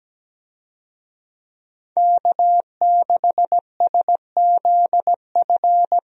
Morse code, heard as a keyed tone, about 17 wpm, 700 Hz. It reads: K6SZF
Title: USA Call Signs - 5 Characters - Repeated 3x - 17wpm